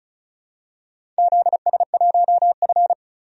Morse code, heard as a keyed tone, about 36 wpm, 700 Hz. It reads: ZS1F
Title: DXCC-100 Call Signs - 4 Characters - 35wpm